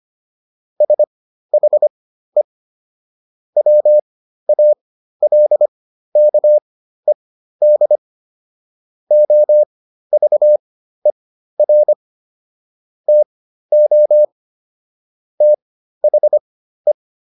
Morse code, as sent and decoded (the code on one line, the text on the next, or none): ... .... / .-- .- .-.. -.- . -.. / --- ...- . .-. / - --- / - .... .
SHE WALKED OVER TO THE